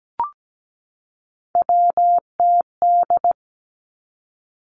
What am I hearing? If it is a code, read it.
Morse code, heard as a keyed tone, about 17 wpm, 700 Hz. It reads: WTD
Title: Three Challenging Letter-Number Combinations 17wpm